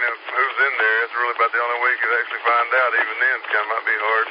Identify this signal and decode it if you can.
Emergency Alert System (SAME) tones, heard as harsh buzzing bursts, with speech. It is heard from 0.5 to 3.6 s.